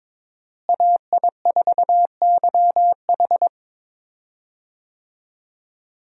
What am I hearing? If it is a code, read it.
Morse code, heard as a keyed tone, about 21 wpm, 700 Hz. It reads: AI4YH
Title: USA Call Signs - 5 Characters - Repeated 2x - 22wpm